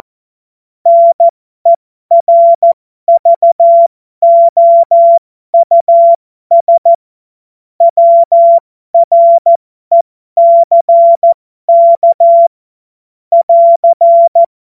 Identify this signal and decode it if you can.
Morse code, heard as a keyed tone, about 14 wpm, spaced as for 12 wpm, 680 Hz. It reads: NERVOUS WRECK <AR>
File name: Morsekurs / Witze